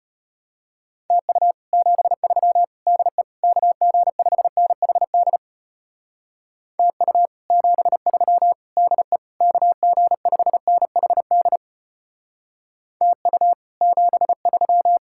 Morse code, heard as a keyed tone, about 32 wpm, 700 Hz. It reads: TU 73 DE KG5NHD TU 73 DE KG5NHD TU 73